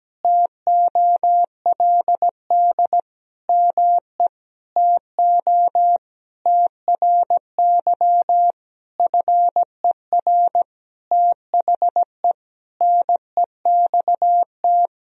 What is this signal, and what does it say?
Morse code, heard as a keyed tone, about 17 wpm, 700 Hz. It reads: TOLD ME TO TRY FER THE NEXT
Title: Sentences from Top 400 Words and Abbreviations 17wpm